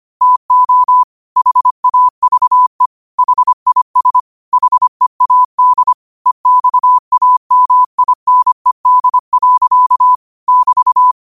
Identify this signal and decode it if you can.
Morse code, heard as a keyed tone, about 25 wpm, 1000 Hz. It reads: TO HAVE HIS HEAD EXAMINED. <BT>
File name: qotdfiles / 25wpm